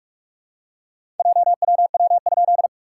Morse code, heard as a keyed tone, about 45 wpm, 690 Hz. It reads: JWW?
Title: SOTA QSO Practice 45wpm